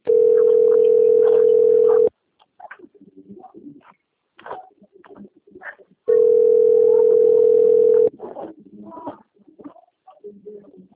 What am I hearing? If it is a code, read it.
Ringback tone (North America), 440 + 480 Hz, with speech.
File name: rg-1600-unknown- - - .11299